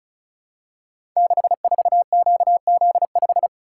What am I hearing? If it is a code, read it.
Morse code, heard as a keyed tone, about 36 wpm, 700 Hz. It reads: 64QZ5
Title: Code Groups - 5 Letter:Numbers - 35wpm